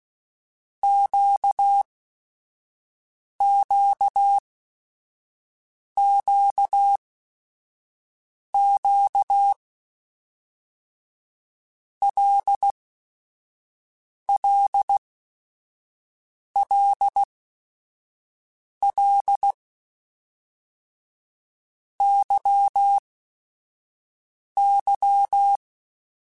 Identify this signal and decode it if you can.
Morse code, heard as a keyed tone, about 16 wpm, 800 Hz. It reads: Q Q Q Q L L L L Y Y
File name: Basic Sets / Basic Letter Groups Collective7 / LFCYQ@5